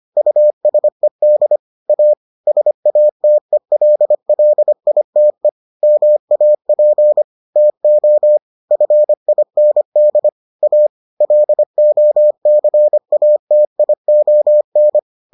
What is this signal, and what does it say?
Morse code, heard as a keyed tone, about 25 wpm, 600 Hz. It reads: USED A SATELLITE MAP TO FIND A LOCATION